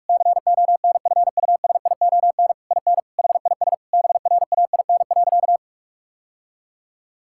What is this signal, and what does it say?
Morse code, heard as a keyed tone, about 46 wpm, 700 Hz. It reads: CONFUSION IN HIS BRAIN.
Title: Sentences from Top 1600 Words 45wpm